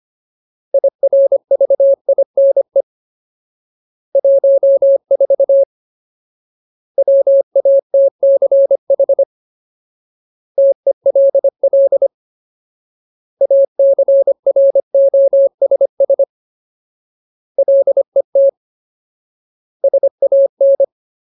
Morse code, heard as a keyed tone, about 25 wpm, 550 Hz. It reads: IRVINE 14 WATCH TELL ACROSS LET SAN
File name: Ultimate Word List / 25WPM Ultimate